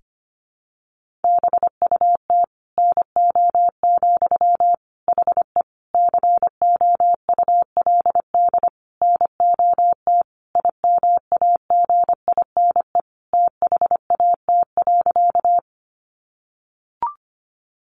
Morse code, heard as a keyed tone, about 25 wpm, 700 Hz. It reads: BUT NO, HE COULD NOT IMAGINE THAT.
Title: Sentences from Top 1400 Words 25wpm